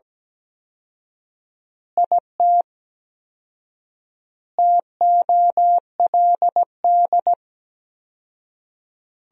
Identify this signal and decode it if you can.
Morse code, heard as a keyed tone, about 17 wpm, 700 Hz. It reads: IT TOLD
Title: 4x Word-Spacing - Sentences from Top 400 Words - 17wpm